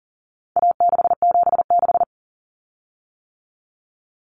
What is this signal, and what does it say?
Morse code, heard as a keyed tone, about 40 wpm, 700 Hz. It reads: A676
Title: Common Cut Numbers - 4-Digit 40wpm